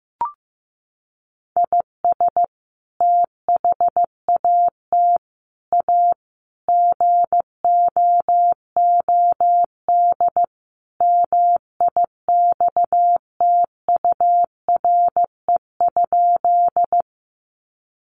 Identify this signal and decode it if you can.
Morse code, heard as a keyed tone, about 15 wpm, 700 Hz. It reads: IS THAT A GOOD MIXTURE?
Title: Sentences from Top 1500 Words 15wpm